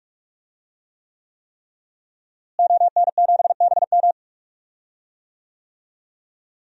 Morse code, heard as a keyed tone, about 46 wpm, 700 Hz. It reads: ON7BM
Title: DXCC-100 Call Signs - 5 Characters - Repeated 3x - 45wpm